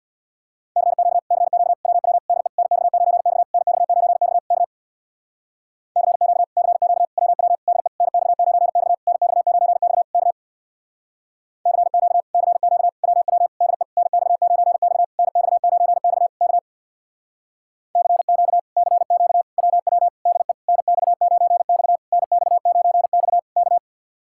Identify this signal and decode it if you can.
Morse code, heard as a keyed tone, about 62 wpm, 700 Hz. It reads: CQ CQ WW DE NX9X NX9X K CQ CQ WW DE NX9X NX9X K CQ CQ WW DE NX9X NX9X K CQ CQ WW DE NX9X NX9X K
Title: Speed-Racing Common QSO Phrases - 50wpm